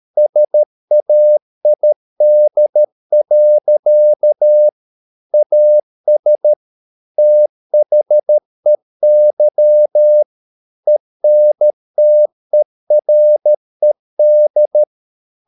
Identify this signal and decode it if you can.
Morse code, heard as a keyed tone, about 13 wpm, 600 Hz. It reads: SAID. AS THEY ENTERED
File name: Never 0013wpm